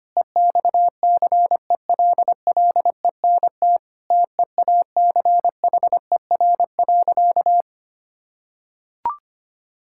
Morse code, heard as a keyed tone, about 25 wpm, 700 Hz. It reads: EXCELLENT TEACHER.
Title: Sentences from Top 1000 Words - Review Part 6 - 25wpm